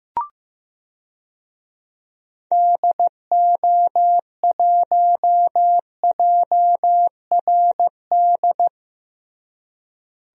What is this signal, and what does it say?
Morse code, heard as a keyed tone, about 15 wpm, 700 Hz. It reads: DO1JRD
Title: DXCC-100 Call Signs - 6 Characters - Repeated 3x - 15wpm